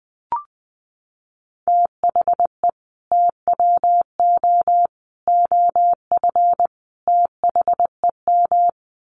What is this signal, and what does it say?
Morse code, heard as a keyed tone, about 20 wpm, 700 Hz. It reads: THE TWO OF THEM
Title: Sets of 4 Words 20wpm